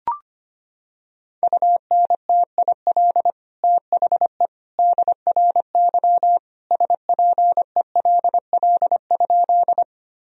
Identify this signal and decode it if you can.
Morse code, heard as a keyed tone, about 25 wpm, 700 Hz. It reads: UNTIL THE DRY SPELL?
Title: Sentences from Top 1000 Words - Review Part 3 - 25wpm